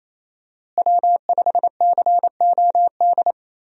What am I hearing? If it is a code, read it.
Morse code, heard as a keyed tone, about 27 wpm, 700 Hz. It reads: W5COD
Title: USA Call Signs - 5 Characters - Repeated 3x - 28wpm